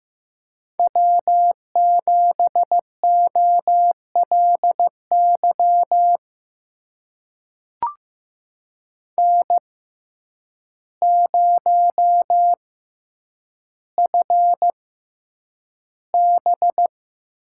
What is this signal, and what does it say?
Morse code, heard as a keyed tone, about 15 wpm, spaced as for 8 wpm, 700 Hz. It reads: W7OLY N0FB
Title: USA Call Signs - 5 Characters - 6x Character Spacing - 15wpm